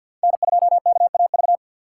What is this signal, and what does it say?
Morse code, heard as a keyed tone, about 48 wpm, 700 Hz. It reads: N2KAV
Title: USA Call Signs - 5 Characters - 2x Character Spacing - 50wpm